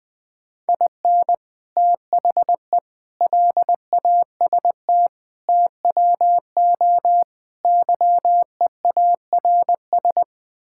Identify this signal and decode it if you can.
Morse code, heard as a keyed tone, about 20 wpm, 700 Hz. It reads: IN THE LAST TWO YEARS
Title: Build A Phrase - 5-Word Phrases - 20wpm